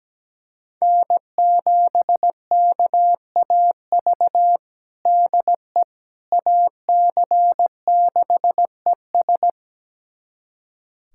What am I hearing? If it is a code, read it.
Morse code, heard as a keyed tone, about 17 wpm, 700 Hz. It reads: N7KAV DE AC6ES
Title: USA Callsign DE Callsign - 5 Characters - 17wpm